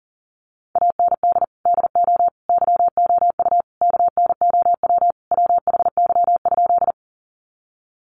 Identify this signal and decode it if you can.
Morse code, heard as a keyed tone, about 40 wpm, 700 Hz. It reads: AND DO YOU KNOW WHY?
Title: Sentences from Top 200 Words 40wpm